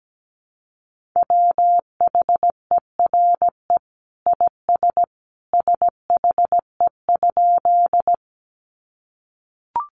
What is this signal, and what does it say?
Morse code, heard as a keyed tone, about 17 wpm, 700 Hz. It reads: WHERE IS SHE?
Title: Sentences from Top 200 Words 17wpm